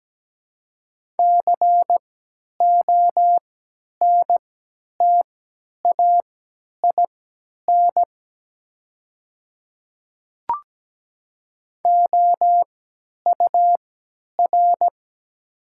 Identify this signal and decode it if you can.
Morse code, heard as a keyed tone, about 17 wpm, spaced as for 10 wpm, 700 Hz. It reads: CONTAIN OUR
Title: 3x Character Spacing - Top 500 Words - 17wpm